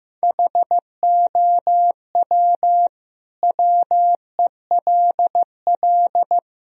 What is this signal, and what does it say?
Morse code, heard as a keyed tone, about 16 wpm, 700 Hz. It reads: HOW WELL
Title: Sentences from Top 1000 Words - Review Part 1 - 15wpm